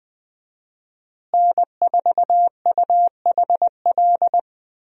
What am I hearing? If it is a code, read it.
Morse code, heard as a keyed tone, about 20 wpm, 700 Hz. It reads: N4UHL
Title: USA Call Signs - 5 Characters - Repeated 3x - 20wpm